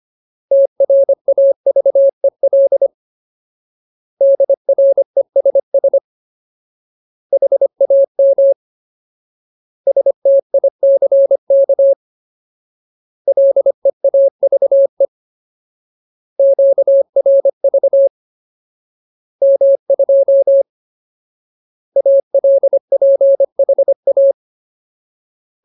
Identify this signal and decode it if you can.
Morse code, heard as a keyed tone, about 25 wpm, 550 Hz. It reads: TRAVEL DRESS HAM STICK LEAVE QRV M2 ALPHA
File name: Ultimate Word List / 25WPM Ultimate